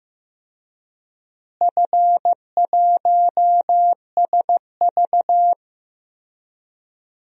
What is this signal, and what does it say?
Morse code, heard as a keyed tone, about 15 wpm, 700 Hz. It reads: F1SV
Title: DXCC-100 Call Signs - 4 Characters - Repeated 3x - 15wpm